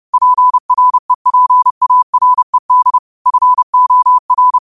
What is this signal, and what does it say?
Morse code, heard as a keyed tone, about 30 wpm, 1000 Hz. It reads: PREPARED FOR